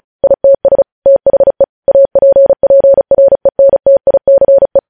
Morse code, heard as a keyed tone, about 36 wpm, 550 Hz. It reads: ITS THE APPRENTICE
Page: news headlines in morse code at 35 wpm